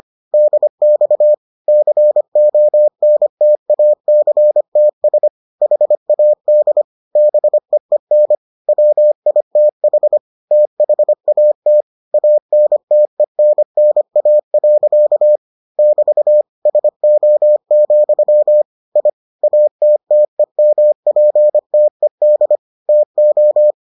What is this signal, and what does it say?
Morse code, heard as a keyed tone, about 25 wpm, 600 Hz. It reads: DX CONTACTS HAD BEEN WITH THAT ANTENNA. <BT> SO, I ATTEMPTED TO